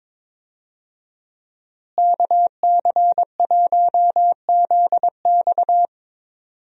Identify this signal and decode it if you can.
Morse code, heard as a keyed tone, about 21 wpm, 700 Hz. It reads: KC1ZX
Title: USA Call Signs - 5 Characters - Repeated 2x - 22wpm